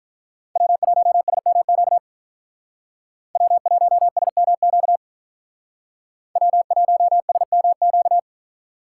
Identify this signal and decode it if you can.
Morse code, heard as a keyed tone, about 45 wpm, 700 Hz. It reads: W1SMQ W1SMQ W1SMQ